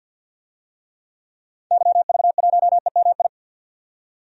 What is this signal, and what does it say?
Morse code, heard as a keyed tone, about 52 wpm, 700 Hz. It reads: YV1EMI
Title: DXCC-100 Call Signs - 6 Characters - 50wpm